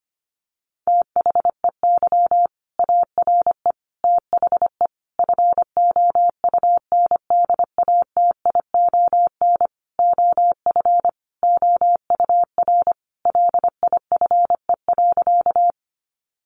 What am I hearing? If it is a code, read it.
Morse code, heard as a keyed tone, about 25 wpm, 700 Hz. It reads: THEY ARE THE FOUNDATION OF OUR LIFE.
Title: Sentences from Top 1200 Words 25wpm